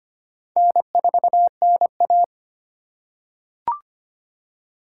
Morse code, heard as a keyed tone, about 25 wpm, 700 Hz. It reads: N4NA
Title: USA Call Signs - 4 Characters - 2x Character Spacing - 25wpm